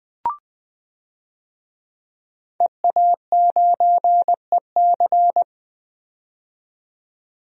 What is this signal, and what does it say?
Morse code, heard as a keyed tone, about 20 wpm, 700 Hz. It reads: EA9EC
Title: DXCC-100 Call Signs - 5 Characters - 20wpm